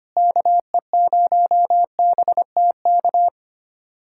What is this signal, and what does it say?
Morse code, heard as a keyed tone, about 25 wpm, 700 Hz. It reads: KE0BTK
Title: USA Callsign DE Callsign - 6 Characters - 25wpm